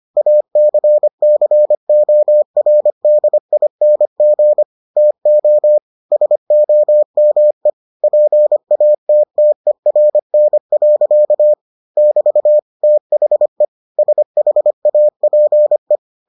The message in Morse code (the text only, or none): ACCORDING TO SOME PATTERN. <BT> THE SHAPE